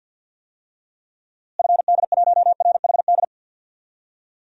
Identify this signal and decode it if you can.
Morse code, heard as a keyed tone, about 47 wpm, 700 Hz. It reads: FD1RHD
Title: DXCC-100 Call Signs - 6 Characters - Repeated 2x - 50wpm